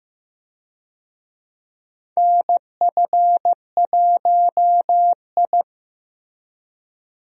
Morse code, heard as a keyed tone, about 15 wpm, 700 Hz. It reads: NF1I
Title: USA Call Signs - 4 Characters - Repeated 2x - 15wpm